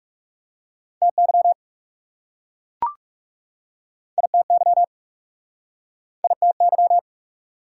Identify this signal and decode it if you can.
Morse code, heard as a keyed tone, about 41 wpm, 700 Hz. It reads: TY ITY ITY